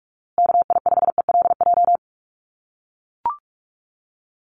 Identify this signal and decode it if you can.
Morse code, heard as a keyed tone, about 44 wpm, 700 Hz. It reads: KI5ELJ
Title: USA Call Signs Repeated 2x 45wpm